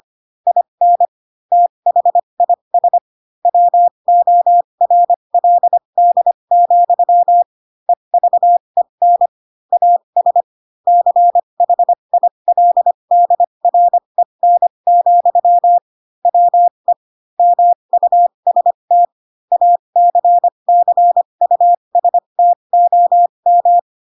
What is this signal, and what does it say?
Morse code, heard as a keyed tone, about 25 wpm, spaced as for 22 wpm, 700 Hz. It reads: IN THIS WORLD, EVEN AS CHILDREN, WE MUST ACCUSTOM
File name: Pinocchio / Pinocchio ch7 18 25 ik1hge